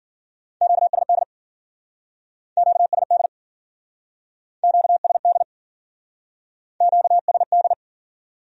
Morse code, heard as a keyed tone, about 49 wpm, 700 Hz. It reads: QSD QSD QSD QSD
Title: Speed-Racing Q-Codes - 40wpm